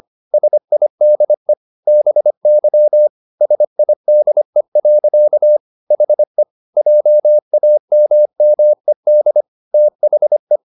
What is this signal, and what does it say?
Morse code, heard as a keyed tone, about 25 wpm, 600 Hz. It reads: SIDE BY SIDE. HE JAMMED THE